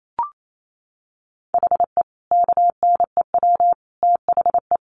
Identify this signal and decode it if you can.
Morse code, heard as a keyed tone, about 28 wpm, 700 Hz. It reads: HE KNEW THE
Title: Sentences from Top 1400 Words 28wpm